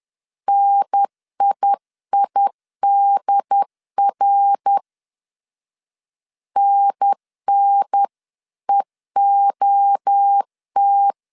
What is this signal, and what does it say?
Morse code, heard as a keyed tone, about 10 wpm, 800 Hz. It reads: NIIDR NNEOT
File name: CW / DET / 1 3 1500C S C 10wpm 3335